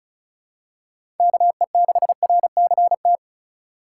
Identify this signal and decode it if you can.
Morse code, heard as a keyed tone, about 36 wpm, 700 Hz. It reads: KE6RCT
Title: USA Call Signs - 6 Characters - 4x Character Spacing - 35wpm